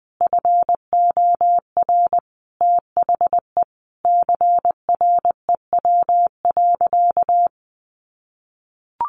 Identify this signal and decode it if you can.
Morse code, heard as a keyed tone, about 19 wpm, 700 Hz. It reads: FOR THE CREW.
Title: Sentences from Top 1500 Words 20wpm